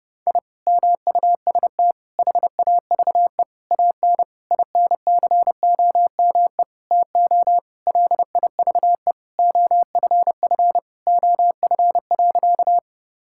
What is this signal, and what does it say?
Morse code, heard as a keyed tone, about 32 wpm, 700 Hz. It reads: I MUST HAVE AN INCOME TO LIVE OFF OF.
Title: Sentences from Top 1000 Words - Review Part 5 - 30wpm